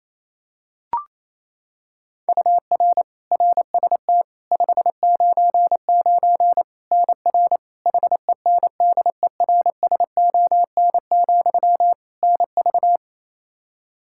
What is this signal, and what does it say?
Morse code, heard as a keyed tone, about 28 wpm, 700 Hz. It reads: UR RST 599 NR HENDERSON, NV
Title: Practice QSO - Exchange 1 - Beginning - 28wpm